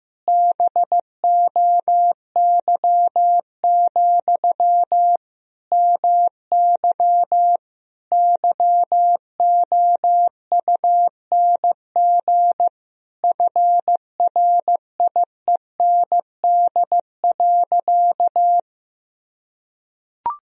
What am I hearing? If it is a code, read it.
Morse code, heard as a keyed tone, about 15 wpm, 700 Hz. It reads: BOY, MY YOUNG FRIEND.
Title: Sentences from Top 1000 Words - Review Part 3 - 15wpm